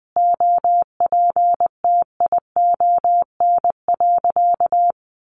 Morse code, heard as a keyed tone, about 20 wpm, 700 Hz. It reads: OPTION.
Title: Sentences from Top 1300 Words 20wpm